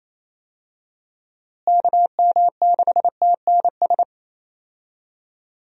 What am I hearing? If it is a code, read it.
Morse code, heard as a keyed tone, about 28 wpm, 700 Hz. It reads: KM6TNS
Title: USA Call Signs - 6 Characters - Repeated 2x - 28wpm